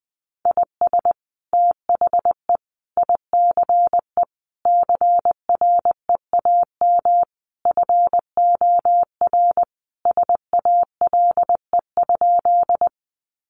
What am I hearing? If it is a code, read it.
Morse code, heard as a keyed tone, about 20 wpm, 700 Hz. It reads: IS THE ICE CREAM FOR SALE?
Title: Sentences from Top 1500 Words 20wpm